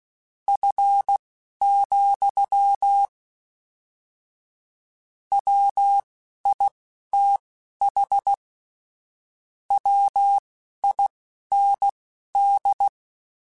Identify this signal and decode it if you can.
Morse code, heard as a keyed tone, about 16 wpm, spaced as for 11 wpm, 800 Hz. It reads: F, WITH WIND
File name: Weather Sets / Weather Sets-mp3s / Weather 4 21 @10